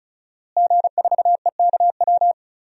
Morse code, heard as a keyed tone, about 36 wpm, 700 Hz. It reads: G4EKW
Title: DXCC-100 Call Signs - 5 Characters - Repeated 3x - 35wpm